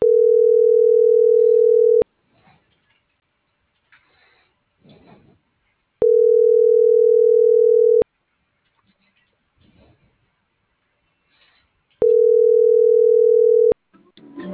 Ringback tone (North America), 440 + 480 Hz.